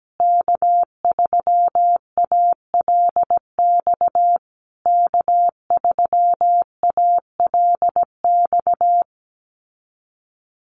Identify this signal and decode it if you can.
Morse code, heard as a keyed tone, about 17 wpm, 700 Hz. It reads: K3ALX K3ALX
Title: USA Call Signs Repeated 3x 17wpm